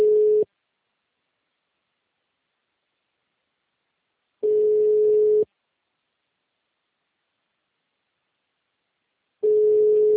Ringback tone (Europe, 425 Hz).